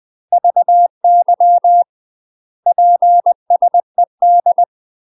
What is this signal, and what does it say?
Morse code, heard as a keyed tone, about 20 wpm, 700 Hz. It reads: VY PSED
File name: UK MASTER QSOs TRAINING / 20WPM / UK MASTER QSOs 0066